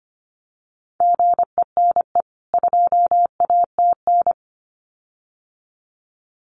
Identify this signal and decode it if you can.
Morse code, heard as a keyed tone, about 25 wpm, 700 Hz. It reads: GENE 2ATN
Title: CWT Exchange 25wpm